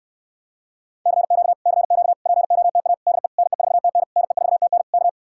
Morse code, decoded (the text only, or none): CQ CQ POTA DE NE4TN NE4TN K